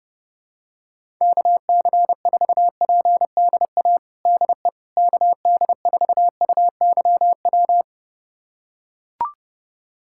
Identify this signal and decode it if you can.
Morse code, heard as a keyed tone, about 32 wpm, 700 Hz. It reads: KC4PDA DE KD4UYW
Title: USA Callsign DE Callsign - 6 Characters - 30wpm